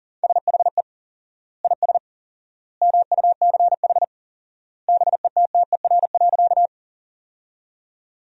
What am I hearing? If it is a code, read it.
Morse code, heard as a keyed tone, about 40 wpm, 700 Hz. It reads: SHE IS MUCH BETTER.
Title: 4x Word-Spacing - Sentences from Top 400 Words - 40wpm